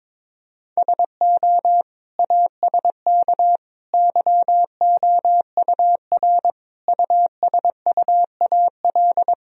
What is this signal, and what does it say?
Morse code, heard as a keyed tone, about 22 wpm, 700 Hz. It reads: SO ASK YOUR USUAL
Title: Sentences from Top 1000 Words - Review Part 2 - 22wpm